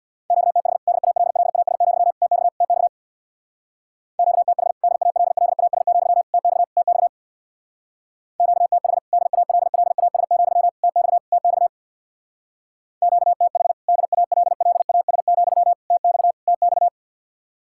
Morse code, heard as a keyed tone, about 62 wpm, 700 Hz. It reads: QTH DALLAS, TX TX QTH DALLAS, TX TX QTH DALLAS, TX TX QTH DALLAS, TX TX
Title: Speed-Racing Common QSO Phrases - 50wpm